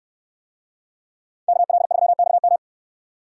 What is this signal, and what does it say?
Morse code, heard as a keyed tone, about 67 wpm, 700 Hz. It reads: KL3XM